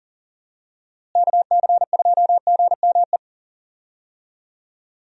Morse code, heard as a keyed tone, about 40 wpm, 700 Hz. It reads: KC2GME
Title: USA Call Signs - 6 Characters - Repeated 2x - 40wpm